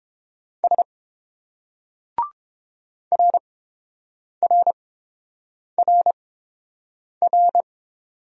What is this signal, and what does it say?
Morse code, heard as a keyed tone, about 31 wpm, spaced as for 7 wpm, 700 Hz. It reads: S RRRR